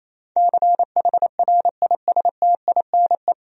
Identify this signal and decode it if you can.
Morse code, heard as a keyed tone, about 28 wpm, 700 Hz. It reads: CHRISTINE
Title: USA Common Names - 28wpm